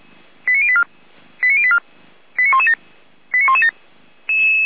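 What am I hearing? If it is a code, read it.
Five-tone selective call (ZVEI), heard as quick runs of five tones, tones of about 70 ms. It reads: 89084 89084 89117 89117
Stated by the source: Five-tone ZVEI selective call received on an SDR.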